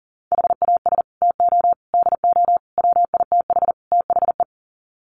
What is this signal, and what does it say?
Morse code, heard as a keyed tone, about 40 wpm, 700 Hz. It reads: HAS TO DO WITH THE
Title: Sets of 5 Words 40wpm